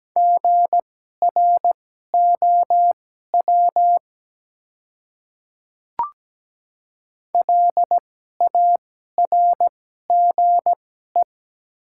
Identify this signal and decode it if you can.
Morse code, heard as a keyed tone, about 17 wpm, spaced as for 12 wpm, 690 Hz. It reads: GROW LARGE
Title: 2x Character Spacing - Top 300 Words - 17wpm